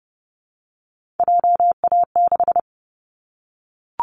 Morse code, heard as a keyed tone, about 30 wpm, 700 Hz. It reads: JA6
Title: Callsign Prefixes for Top 50 DXCC - 30wpm